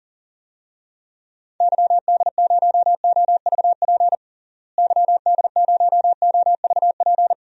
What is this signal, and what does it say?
Morse code, heard as a keyed tone, about 37 wpm, 700 Hz. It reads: YD0OVP YD0OVP
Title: DXCC-100 Call Signs - 6 Characters - Repeated 3x - 40wpm